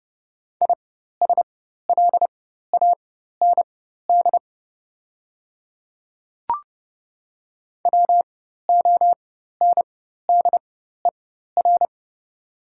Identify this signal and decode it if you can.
Morse code, heard as a keyed tone, about 30 wpm, spaced as for 14 wpm, 690 Hz. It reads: ISLAND WONDER
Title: 4x Character Spacing - Top 500 Words - 30wpm